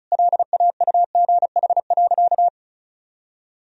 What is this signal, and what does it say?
Morse code, heard as a keyed tone, about 36 wpm, 700 Hz. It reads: LAUGH.
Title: Sentences from Top 1000 Words - Review Part 1 - 35wpm